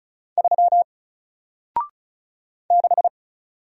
Morse code, heard as a keyed tone, about 34 wpm, 700 Hz. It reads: 3 6
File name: Single-Letter-Number-35wpm